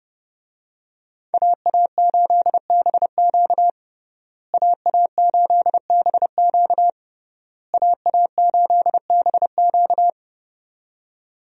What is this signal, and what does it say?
Morse code, heard as a keyed tone, about 30 wpm, 700 Hz. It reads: AA8BQ AA8BQ AA8BQ
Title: USA Call Signs - 5 Characters - Repeated 3x - 30wpm